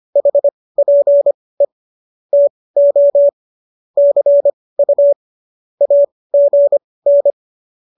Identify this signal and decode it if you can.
Morse code, heard as a keyed tone, about 25 wpm, spaced as for 18 wpm, 570 Hz. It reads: HPE TO CU AGN